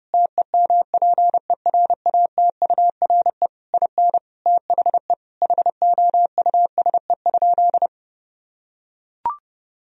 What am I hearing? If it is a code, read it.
Morse code, heard as a keyed tone, about 30 wpm, 700 Hz. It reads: TEMPERATURE IN THE HOUSE?
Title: Sentences from Top 1000 Words - Review Part 4 - 30wpm